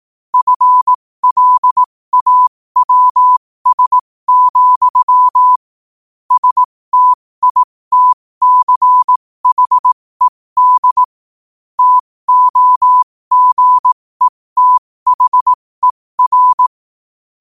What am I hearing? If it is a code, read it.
Morse code, heard as a keyed tone, about 18 wpm, spaced as for 15 wpm, 1000 Hz. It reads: FLAWS, STITCHED TOGETHER
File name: qotdfiles / 15wpm